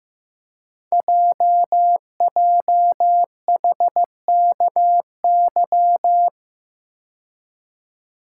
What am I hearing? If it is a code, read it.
Morse code, heard as a keyed tone, about 15 wpm, 700 Hz. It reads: JJHKY
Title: Code Groups - 5 Letters - 15wpm